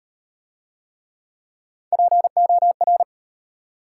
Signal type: Morse code

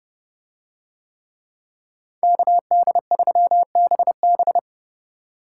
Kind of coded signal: Morse code

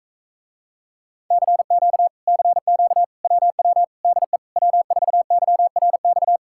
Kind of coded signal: Morse code

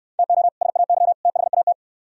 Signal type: Morse code

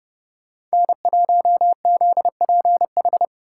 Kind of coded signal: Morse code

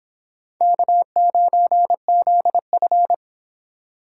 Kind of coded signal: Morse code

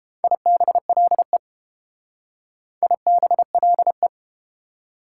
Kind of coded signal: Morse code

